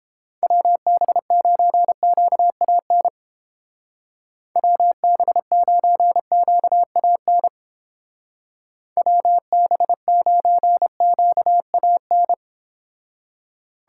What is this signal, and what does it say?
Morse code, heard as a keyed tone, about 31 wpm, 700 Hz. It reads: WB9QAN WB9QAN WB9QAN